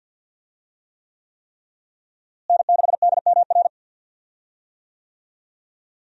Morse code, heard as a keyed tone, about 52 wpm, 700 Hz. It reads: N6DMR
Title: USA Call Signs - 5 Characters - Repeated 3x - 50wpm